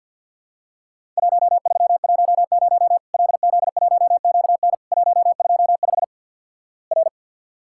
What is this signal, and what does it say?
Morse code, heard as a keyed tone, about 50 wpm, 700 Hz. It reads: 1310 LZ1QN 125 R
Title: Worked All Europe DX Contest - QTC Exchange - 50wpm